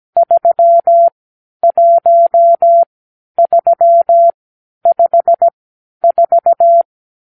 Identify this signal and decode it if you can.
Morse code, heard as a keyed tone, about 17 wpm, spaced as for 10 wpm, 680 Hz. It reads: 31354
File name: k7qo / Lesson 50 (1-6)